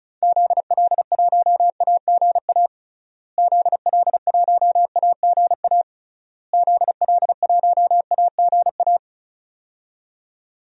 Morse code, heard as a keyed tone, about 34 wpm, 700 Hz. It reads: ZL1AGA ZL1AGA ZL1AGA
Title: DXCC-100 Call Signs - 6 Characters - Repeated 3x - 35wpm